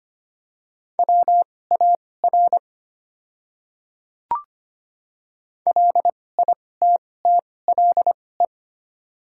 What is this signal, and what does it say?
Morse code, heard as a keyed tone, about 25 wpm, spaced as for 18 wpm, 700 Hz. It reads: WAR LITTLE
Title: 2x Character Spacing - Top 500 Words - 25wpm